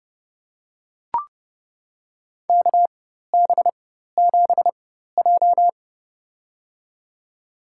Morse code, heard as a keyed tone, about 30 wpm, spaced as for 14 wpm, 700 Hz. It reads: KB7J
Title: USA Call Signs - 4x Character Spacing - 30wpm